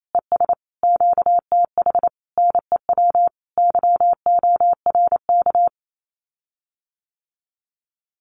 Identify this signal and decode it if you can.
Morse code, heard as a keyed tone, about 28 wpm, 700 Hz. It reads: ES QTH NEW YORK
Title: Common QSO Phrases Encore 28wpm